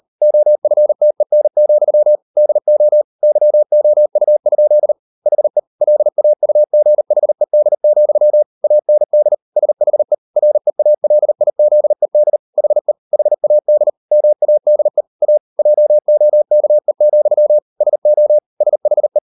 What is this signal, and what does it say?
Morse code, heard as a keyed tone, about 38 wpm, 600 Hz. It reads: OFTEN, DO YOU? HE LAUGHED, AND SHE REALIZED HE HAD MADE A JOKE, SO SHE